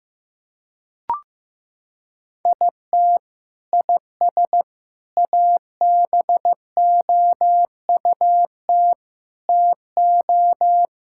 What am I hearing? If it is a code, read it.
Morse code, heard as a keyed tone, about 15 wpm, 700 Hz. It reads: IT IS ABOUT TO
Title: Sentences from Top 1600 Words 15wpm